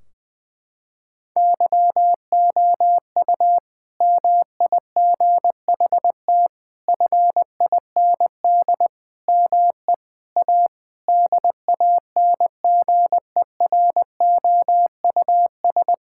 Morse code, heard as a keyed tone, about 20 wpm, 700 Hz. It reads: YOU MIGHT FIND ME A DANGEROUS
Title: Sentences from Top 1600 Words 20wpm